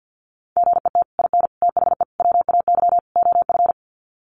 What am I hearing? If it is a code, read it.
Morse code, heard as a keyed tone, about 50 wpm, 700 Hz. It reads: GET IN THE WAY OF
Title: Sets of 5 Words Encore 50wpm